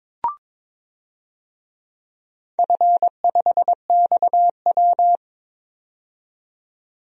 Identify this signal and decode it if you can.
Morse code, heard as a keyed tone, about 21 wpm, 700 Hz. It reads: F5XW
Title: DXCC-100 Call Signs - 4 Characters - 22wpm